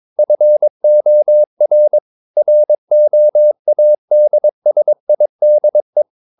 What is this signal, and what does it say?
Morse code, heard as a keyed tone, about 22 wpm, 600 Hz. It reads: FOR ROADSIDE
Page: Romanen Never af Ken Follet på engelsk.